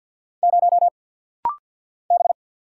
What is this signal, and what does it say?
Morse code, heard as a keyed tone, about 49 wpm, 700 Hz. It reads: 0 B
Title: Single Letter-Number - Rapid-Fire 50wpm